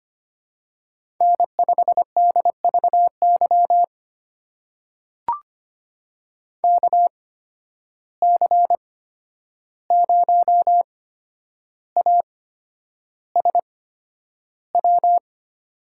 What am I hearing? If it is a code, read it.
Morse code, heard as a keyed tone, about 25 wpm, spaced as for 10 wpm, 700 Hz. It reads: N5DVY KC0ASW
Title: USA Call Signs - 8x Character Spacing - 25wpm